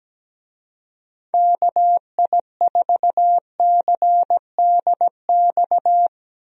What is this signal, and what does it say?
Morse code, heard as a keyed tone, about 17 wpm, 700 Hz. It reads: KI4CDX
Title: USA Call Signs - 6 Characters - Repeated 2x - 17wpm